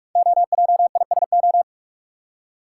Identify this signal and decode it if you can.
Morse code, heard as a keyed tone, about 44 wpm, 690 Hz. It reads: OJISO
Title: Code Groups - 5 Letters - 45wpm